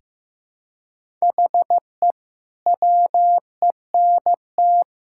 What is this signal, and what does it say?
Morse code, heard as a keyed tone, about 15 wpm, 700 Hz. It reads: HE WENT
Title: Sentences from Top 1600 Words 15wpm